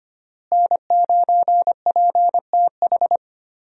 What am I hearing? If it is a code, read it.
Morse code, heard as a keyed tone, about 25 wpm, 700 Hz. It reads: N9PTH
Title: USA Call Signs - 5 Characters - Repeated 2x - 25wpm